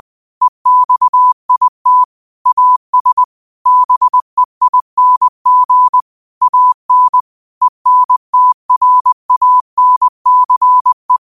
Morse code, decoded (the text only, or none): EXIT AS BEING AN ENTRANCE